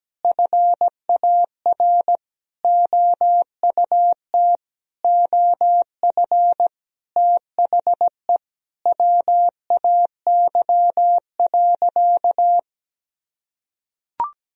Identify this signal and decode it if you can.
Morse code, heard as a keyed tone, about 17 wpm, 700 Hz. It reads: FAR OUT OF THE WAY.
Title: Sentences from Top 1000 Words - Review Part 1 - 17wpm